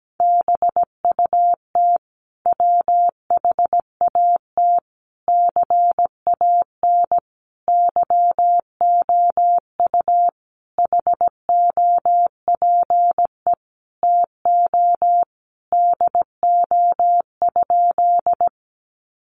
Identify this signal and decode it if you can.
Morse code, heard as a keyed tone, about 17 wpm, 700 Hz. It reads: BUT WHAT CAN YOU HOPE TO DO?
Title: Sentences from Top 1000 Words 17wpm